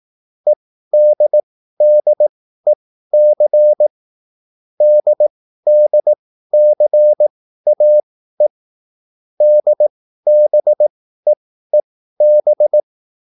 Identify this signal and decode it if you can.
Morse code, heard as a keyed tone, about 18 wpm, spaced as for 13 wpm, 600 Hz. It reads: EDDEC DDCAE DBEEB